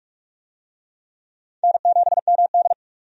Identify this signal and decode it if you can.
Morse code, heard as a keyed tone, about 46 wpm, 700 Hz. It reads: N7MD
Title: USA Call Signs - 4 Characters - 45wpm